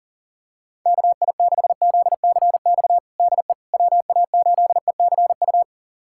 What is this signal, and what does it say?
Morse code, heard as a keyed tone, about 40 wpm, 700 Hz. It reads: KI6ZCX DE WA8ECU
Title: USA Callsign DE Callsign - 6 Characters - 40wpm